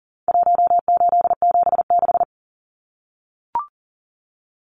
Morse code, heard as a keyed tone, about 40 wpm, 700 Hz. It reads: 1876
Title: Numbers - Contest Style - 4-Digit 40wpm